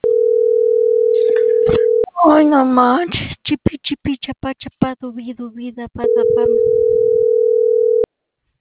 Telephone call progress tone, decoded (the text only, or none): ringback tone